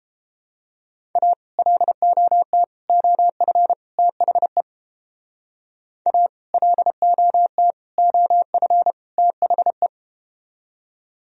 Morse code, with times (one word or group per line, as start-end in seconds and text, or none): 1.2-1.3 A
1.6-2.6 LOT
2.9-3.7 OF
4.0-4.6 THE
6.1-6.3 A
6.5-7.7 LOT
8.0-8.9 OF
9.2-9.9 THE